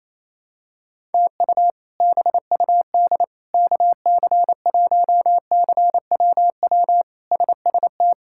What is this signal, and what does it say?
Morse code, heard as a keyed tone, about 27 wpm, 700 Hz. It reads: TU BUD KC1CWW SST
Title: K1USN SST QSO Practice - 28wpm